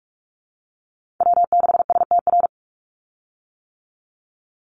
Morse code, heard as a keyed tone, about 44 wpm, 700 Hz. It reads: W6STR
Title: Call Signs - 45wpm